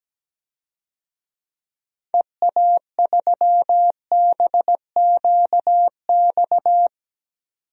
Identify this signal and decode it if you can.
Morse code, heard as a keyed tone, about 17 wpm, 700 Hz. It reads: EA3BQX
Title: DXCC-100 Call Signs - 6 Characters - 17wpm